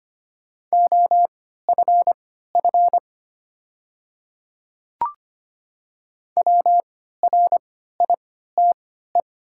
Morse code, heard as a keyed tone, about 25 wpm, spaced as for 14 wpm, 700 Hz. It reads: OFF WRITE
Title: 3x Character Spacing - Top 200 Words - 25wpm